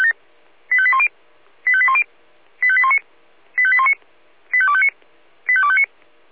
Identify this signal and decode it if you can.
Five-tone selective call (ZVEI), heard as quick runs of five tones, tones of about 70 ms. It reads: 85810 85810 85819 85819 85259 85259